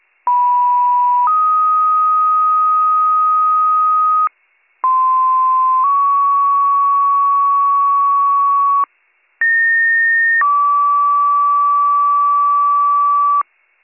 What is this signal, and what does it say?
Two-tone page, heard as a one-second tone, then a three-second tone, about 980 then 1250 Hz, about 1040 then 1120 Hz, about 1850 then 1150 Hz. It is heard from 0.3 to 13.4 s.